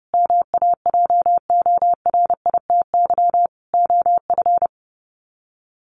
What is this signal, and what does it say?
Morse code, heard as a keyed tone, about 30 wpm, 700 Hz. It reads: MAJORITY OF
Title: Sets of 4 Words 30wpm